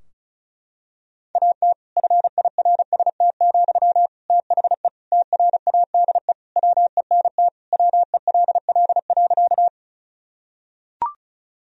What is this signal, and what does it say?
Morse code, heard as a keyed tone, about 36 wpm, 700 Hz. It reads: AT FIRST, THE TRADE WENT WELL.
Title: Sentences from Top 1000 Words - Review Part 4 - 35wpm